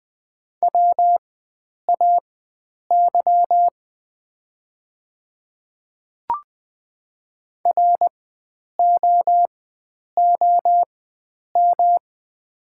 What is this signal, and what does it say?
Morse code, heard as a keyed tone, about 20 wpm, spaced as for 9 wpm, 700 Hz. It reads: WAY ROOM
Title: 4x Character Spacing - Top 500 Words - 20wpm